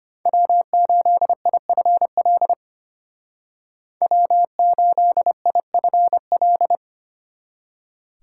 Morse code, heard as a keyed tone, about 32 wpm, 700 Hz. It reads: W8IFL W8IFL